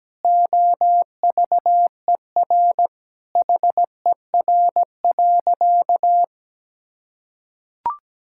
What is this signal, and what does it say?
Morse code, heard as a keyed tone, about 17 wpm, 700 Hz. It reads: OVER HER.
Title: Sentences from Top 1000 Words - Review Part 5 - 17wpm